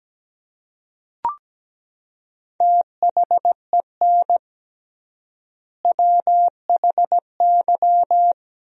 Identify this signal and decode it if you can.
Morse code, heard as a keyed tone, about 17 wpm, 700 Hz. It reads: THEN WHY
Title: 3x Word-Spacing - Sentences from Top 200 Words - 17wpm